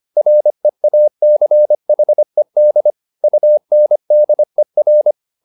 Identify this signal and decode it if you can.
Morse code, heard as a keyed tone, about 25 wpm, 600 Hz. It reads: REACHED UNDER